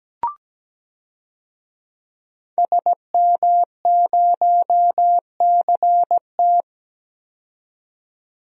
Morse code, heard as a keyed tone, about 17 wpm, 700 Hz. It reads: SM0CT
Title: DXCC-100 Call Signs - 5 Characters - 17wpm